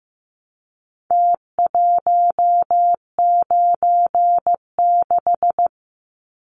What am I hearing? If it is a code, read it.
Morse code, heard as a keyed tone, about 15 wpm, 700 Hz. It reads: T196
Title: Numbers - Contest Style - 4-Digit 15wpm